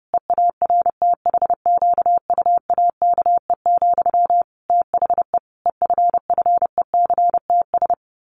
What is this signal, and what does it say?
Morse code, heard as a keyed tone, about 30 wpm, 700 Hz. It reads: EARTHQUAKE, THE EFFECTS
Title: A Sound Of Thunder 30wpm